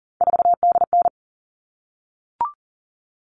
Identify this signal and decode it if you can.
Morse code, heard as a keyed tone, about 38 wpm, 700 Hz. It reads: 4DN
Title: All Cut Numbers - 3-Digit 40wpm